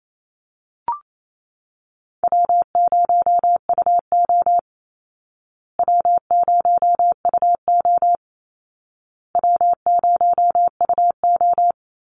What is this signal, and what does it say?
Morse code, heard as a keyed tone, about 28 wpm, 700 Hz. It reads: W0UO W0UO W0UO
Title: Call Signs Repeated 2x - 28wpm